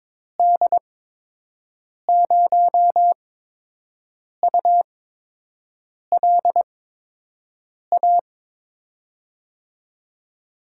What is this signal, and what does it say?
Morse code, heard as a keyed tone, about 22 wpm, spaced as for 6 wpm, 700 Hz. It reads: D0ULA
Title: USA Call Signs - 6 Characters - 8x Character Spacing - 22wpm